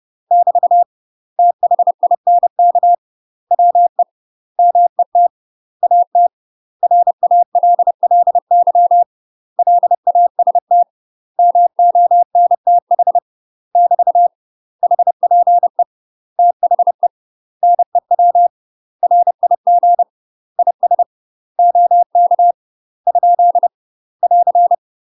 Morse code, heard as a keyed tone, about 30 wpm, 700 Hz. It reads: <BT> THINK WE MET AT RALLY LAST MONTH <BT> HPE THE NEW RIG IS OK ? <AR>
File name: UK MASTER QSOs TRAINING / 30WPM / UK MASTER QSOs 0003